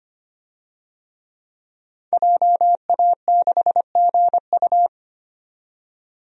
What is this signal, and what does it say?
Morse code, heard as a keyed tone, about 25 wpm, 700 Hz. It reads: JA6GU
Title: DXCC-100 Call Signs - 5 Characters - 25wpm